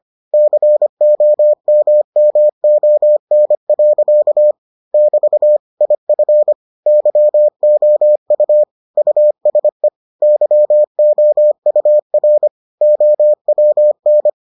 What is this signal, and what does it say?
Morse code, heard as a keyed tone, about 25 wpm, 600 Hz. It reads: COMMON. <BT> IF YOU USE YOUR OWN